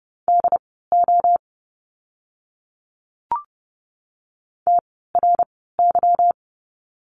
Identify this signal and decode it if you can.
Morse code, heard as a keyed tone, about 30 wpm, spaced as for 17 wpm, 700 Hz. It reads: DO TRY
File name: course / 30:10wpm / 230 Lesson 37f - Letter x - Words 30wpm